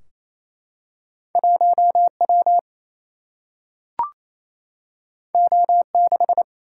Morse code, heard as a keyed tone, about 28 wpm, 700 Hz. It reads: 1 W O 6
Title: Single-Letter Single-Number 28wpm